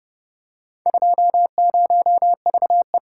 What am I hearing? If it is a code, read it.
Morse code, heard as a keyed tone, about 29 wpm, 700 Hz. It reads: 20VE